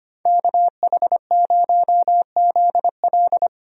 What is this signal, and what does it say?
Morse code, heard as a keyed tone, about 25 wpm, 700 Hz. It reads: KH0ZL
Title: USA Call Signs - 5 Characters - 25wpm